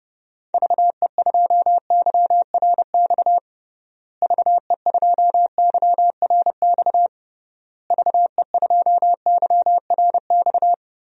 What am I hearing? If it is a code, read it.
Morse code, heard as a keyed tone, about 29 wpm, 700 Hz. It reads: VE2YRX VE2YRX VE2YRX
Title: DXCC-100 Call Signs - 6 Characters - Repeated 3x - 30wpm